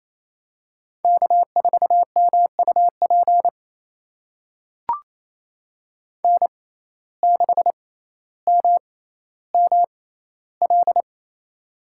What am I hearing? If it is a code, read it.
Morse code, heard as a keyed tone, about 28 wpm, spaced as for 14 wpm, 700 Hz. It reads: K4MUP N6MML
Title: USA Call Signs - 5 Characters - 6x Character Spacing - 28wpm